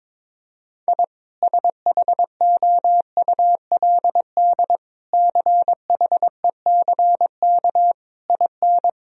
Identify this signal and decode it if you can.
Morse code, heard as a keyed tone, about 22 wpm, 700 Hz. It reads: I SHOULD CHECK IN
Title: Sentences from Top 1000 Words - Review Part 3 - 22wpm